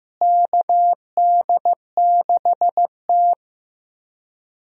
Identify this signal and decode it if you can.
Morse code, heard as a keyed tone, about 15 wpm, 700 Hz. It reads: KD6T
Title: USA Call Signs - 4 Characters - Repeated 2x - 15wpm